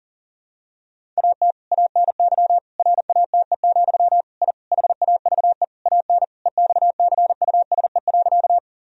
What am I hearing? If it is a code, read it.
Morse code, heard as a keyed tone, about 40 wpm, 700 Hz. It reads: AT ANY RATE, I HAVE AN EXCUSE.
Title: Sentences from Top 1000 Words - Review Part 4 - 40wpm